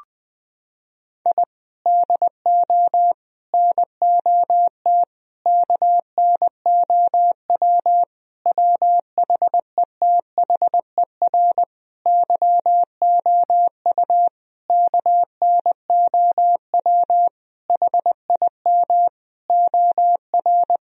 Morse code, heard as a keyed tone, about 20 wpm, 700 Hz. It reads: I DO NOT KNOW WHETHER YOU KNOW HIM OR
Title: Sentences from Top 1000 Words - Review Part 3 - 20wpm